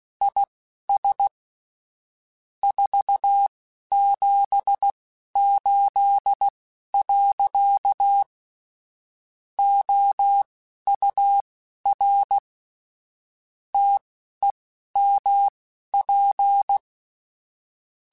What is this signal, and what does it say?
Morse code, heard as a keyed tone, about 16 wpm, spaced as for 11 wpm, 800 Hz. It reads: IS 478. OUR TEMP
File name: QSOs / QSOs-mp3s / WD0BGP 2@10